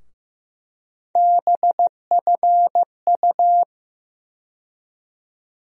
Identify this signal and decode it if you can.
Morse code, heard as a keyed tone, about 15 wpm, 700 Hz. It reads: BFU
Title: USA Call Sign Postfixes - 3 Letters 15wpm